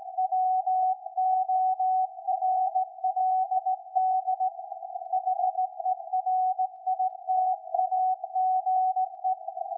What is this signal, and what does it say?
Morse code, heard as a keyed tone, about 17 wpm, 750 Hz. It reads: WORLD HERITAGE
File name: iz7slz / VLF / SAQ 13-02-2023